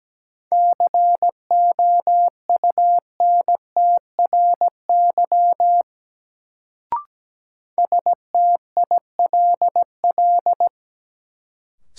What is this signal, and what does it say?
Morse code, heard as a keyed tone, about 17 wpm, 700 Hz. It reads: COUNTRY STILL
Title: Sentences from Top 300 Words - New Word Drill - 17wpm